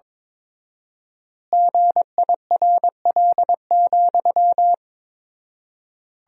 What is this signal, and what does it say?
Morse code, heard as a keyed tone, about 22 wpm, 700 Hz. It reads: GIRL,
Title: 4x Word-Spacing - Sentences from Top 400 Words - 22wpm